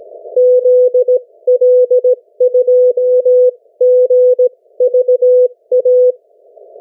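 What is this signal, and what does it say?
Morse code, heard as a keyed tone, about 17 wpm, 500 Hz. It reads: ZL2GVA
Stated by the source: Morse code from the national amateur radio CW net of New Zealand, which meets on 3535.0 kHz or 7030 kHz.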